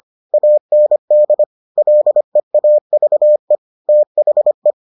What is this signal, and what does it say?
Morse code, heard as a keyed tone, about 25 wpm, 600 Hz. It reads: AND LEAVE THE